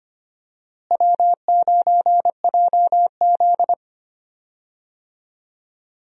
Morse code, heard as a keyed tone, about 25 wpm, 700 Hz. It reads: W9JZ
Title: USA Call Signs - 4 Characters - Repeated 2x - 25wpm